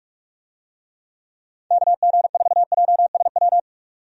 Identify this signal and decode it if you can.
Morse code, heard as a keyed tone, about 46 wpm, 700 Hz. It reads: KG4JSW
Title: USA Call Signs - 6 Characters - 45wpm